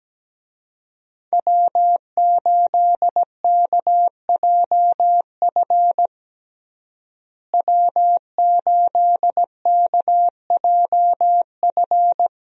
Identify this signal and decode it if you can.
Morse code, heard as a keyed tone, about 17 wpm, 700 Hz. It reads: W8KJF W8KJF
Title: USA Call Signs - 5 Characters - Repeated 3x - 17wpm